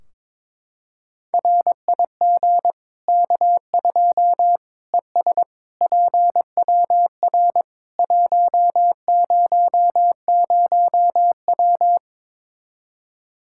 Morse code, heard as a keyed tone, about 22 wpm, 700 Hz. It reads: RIG K2 ES PWR 100W
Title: Practice QSO - Exchange 2 - Info - 22wpm